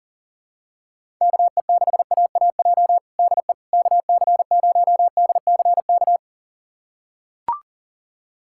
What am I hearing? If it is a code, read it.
Morse code, heard as a keyed tone, about 37 wpm, 700 Hz. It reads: KE6AAJ DE KC0DCK
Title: USA Callsign DE Callsign - 6 Characters - 40wpm